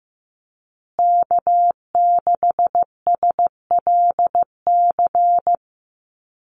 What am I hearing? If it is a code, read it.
Morse code, heard as a keyed tone, about 15 wpm, 700 Hz. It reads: K6SLC
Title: USA Call Signs Repeated 2x 15wpm